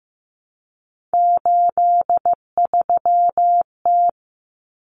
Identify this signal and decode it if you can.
Morse code, heard as a keyed tone, about 15 wpm, 700 Hz. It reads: 83T
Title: Common Cut Numbers - 3-Digit 15wpm